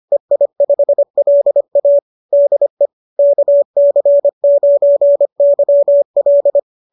Morse code, heard as a keyed tone, about 25 wpm, 590 Hz. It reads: EI5LA DE KC9YL